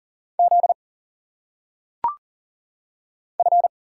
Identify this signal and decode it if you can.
Morse code, heard as a keyed tone, about 38 wpm, 700 Hz. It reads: Z F
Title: Single Letters - Phonetic Pronunciation 40wpm